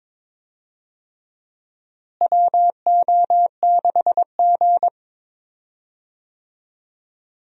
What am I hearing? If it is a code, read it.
Morse code, heard as a keyed tone, about 21 wpm, 700 Hz. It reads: WO6G
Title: USA Call Signs - 4 Characters - Repeated 3x - 22wpm